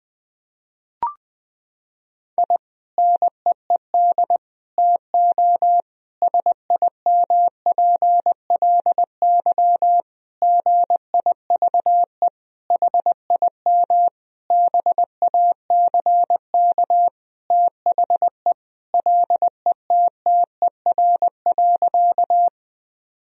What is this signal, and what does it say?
Morse code, heard as a keyed tone, about 20 wpm, 700 Hz. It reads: I NEED TO SIMPLY GIVE HIM BACK THE LETTER.
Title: Sentences from Top 1000 Words - Review Part 4 - 20wpm